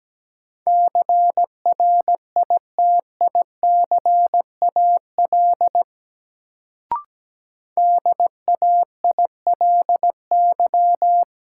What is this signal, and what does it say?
Morse code, heard as a keyed tone, about 17 wpm, 700 Hz. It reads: CRITICAL DAILY
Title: Sentences from Top 1100 Words - New Word Drill Alt - 17wpm